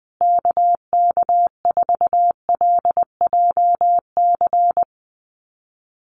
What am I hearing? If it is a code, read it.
Morse code, heard as a keyed tone, about 20 wpm, 700 Hz. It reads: KK4LJC
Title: USA Call Signs Repeated 3x 20wpm